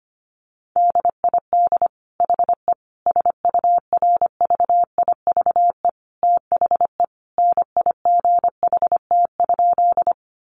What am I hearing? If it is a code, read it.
Morse code, heard as a keyed tone, about 25 wpm, 700 Hz. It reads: DID HE SURVIVE THE NIGHT?
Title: Sentences from Top 1300 Words 25wpm